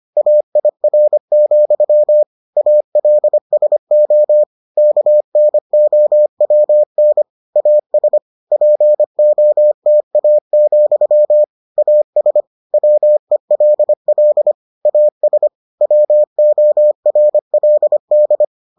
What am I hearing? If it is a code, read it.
Morse code, heard as a keyed tone, about 25 wpm, 600 Hz. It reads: AIR, ALSO KNOWN AS POTA, AS WELL AS WORLD